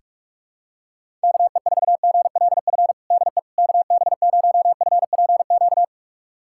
Morse code, heard as a keyed tone, about 41 wpm, 700 Hz. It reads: KE4GLF DE KB0FPQ
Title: USA Callsign DE Callsign 45wpm